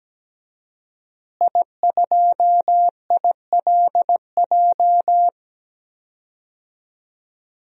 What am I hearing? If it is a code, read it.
Morse code, heard as a keyed tone, about 17 wpm, 700 Hz. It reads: I2ILJ
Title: DXCC-100 Call Signs - 5 Characters - Repeated 2x - 17wpm